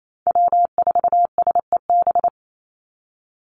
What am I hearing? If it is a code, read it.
Morse code, heard as a keyed tone, about 28 wpm, 700 Hz. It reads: W4SEB
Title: USA Call Signs 28wpm